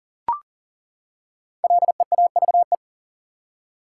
Morse code, heard as a keyed tone, about 40 wpm, 700 Hz. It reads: LEAVE
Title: Five Letter Words - 40wpm